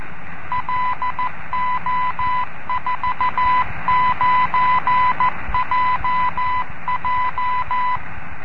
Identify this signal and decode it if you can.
Morse code, heard as a keyed tone, about 14 wpm, 1000 Hz, with noise. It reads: LO49JJ